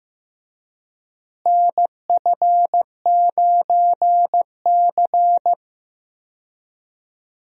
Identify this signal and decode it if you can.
Morse code, heard as a keyed tone, about 15 wpm, 700 Hz. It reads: NF9C
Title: USA Call Signs - 4 Characters - Repeated 3x - 15wpm